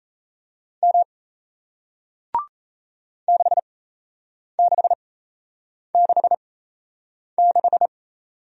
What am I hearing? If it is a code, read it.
Morse code, heard as a keyed tone, about 38 wpm, spaced as for 8 wpm, 700 Hz. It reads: M 6666